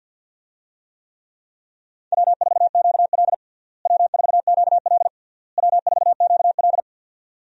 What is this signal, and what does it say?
Morse code, heard as a keyed tone, about 52 wpm, 700 Hz. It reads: W4QL W4QL W4QL
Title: USA Call Signs - 4 Characters - Repeated 3x - 50wpm